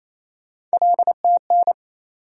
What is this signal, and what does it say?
Morse code, heard as a keyed tone, about 28 wpm, 700 Hz. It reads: LTN
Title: USA Call Sign Postfixes - 3 Letters 28wpm